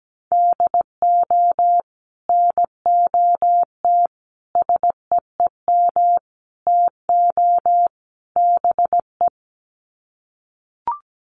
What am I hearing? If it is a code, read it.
Morse code, heard as a keyed tone, about 17 wpm, 700 Hz. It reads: DO NOT SEEM TO BE
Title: Sets of 5 Words Encore2 17wpm